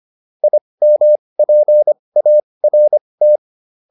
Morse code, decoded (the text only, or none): IMPART